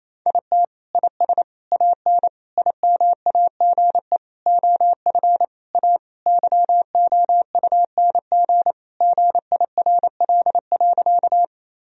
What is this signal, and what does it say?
Morse code, heard as a keyed tone, about 28 wpm, 700 Hz. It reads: IT IS AN IMAGE OF A YOUNG GIRL.
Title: Sentences from Top 1000 Words - Review Part 4 - 28wpm